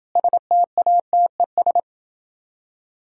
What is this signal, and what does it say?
Morse code, heard as a keyed tone, about 29 wpm, 700 Hz. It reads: STATES